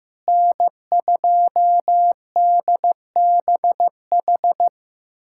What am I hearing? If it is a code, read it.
Morse code, heard as a keyed tone, about 15 wpm, 700 Hz. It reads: N2DBH
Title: USA Callsign DE Callsign - 5 Characters - 15wpm